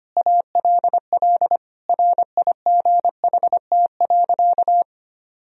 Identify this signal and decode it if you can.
Morse code, heard as a keyed tone, about 25 wpm, 700 Hz. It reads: ALL RIGHT.
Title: Sentences from Top 1000 Words - Review Part 4 - 25wpm